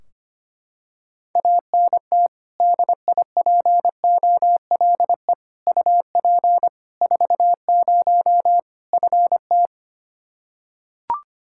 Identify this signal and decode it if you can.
Morse code, heard as a keyed tone, about 25 wpm, 700 Hz. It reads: ANT DIPOLE UP 40 FT
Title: Practice QSO - Exchange 2 - Info - 25wpm